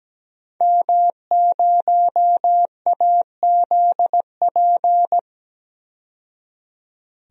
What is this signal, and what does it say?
Morse code, heard as a keyed tone, about 17 wpm, 700 Hz. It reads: M0AZP
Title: DXCC-100 Call Signs - 5 Characters - 17wpm